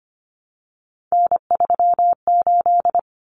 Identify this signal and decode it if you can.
Morse code, heard as a keyed tone, about 25 wpm, 700 Hz. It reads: N38
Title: Common Cut Numbers - 3-Digit 25wpm